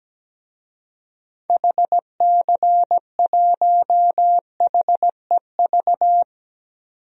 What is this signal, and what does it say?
Morse code, heard as a keyed tone, about 17 wpm, 700 Hz. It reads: HC1HEV
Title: DXCC-100 Call Signs - 6 Characters - 17wpm